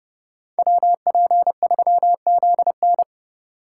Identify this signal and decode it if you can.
Morse code, heard as a keyed tone, about 29 wpm, 700 Hz. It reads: WP3ZN